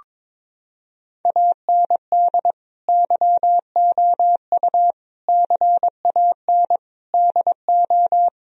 Morse code, heard as a keyed tone, about 22 wpm, 700 Hz. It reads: AND YOU CAN DO
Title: Sentences from Top 1000 Words - Review Part 1 - 22wpm